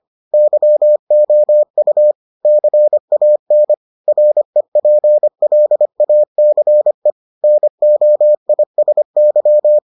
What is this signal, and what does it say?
Morse code, heard as a keyed tone, about 25 wpm, 600 Hz. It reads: YOU CAN REPLACE NOISY